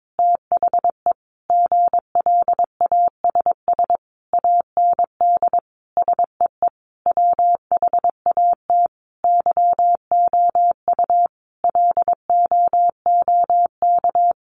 Morse code, heard as a keyed tone, about 22 wpm, 700 Hz. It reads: THE GLASS AND SEE WHAT YOU LOOK
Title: Sentences from Top 1300 Words 22wpm